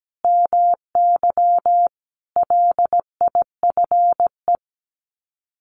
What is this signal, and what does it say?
Morse code, heard as a keyed tone, about 17 wpm, 700 Hz. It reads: MY LIFE
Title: Sets of 5 Words 17wpm